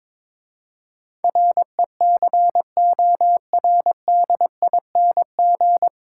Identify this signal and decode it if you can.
Morse code, heard as a keyed tone, about 22 wpm, 700 Hz. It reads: RECORDING
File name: 9 Letter Words - First 3 Letters Given - 22wpm